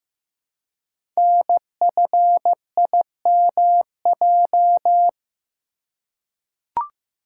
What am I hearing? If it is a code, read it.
Morse code, heard as a keyed tone, about 15 wpm, 700 Hz. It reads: NFIMJ
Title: Code Groups - 5 Letters - 15wpm